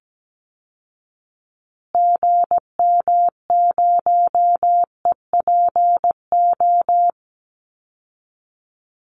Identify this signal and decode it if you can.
Morse code, heard as a keyed tone, about 17 wpm, 700 Hz. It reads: GM0EPO
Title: Call Signs Repeated 2x - 17wpm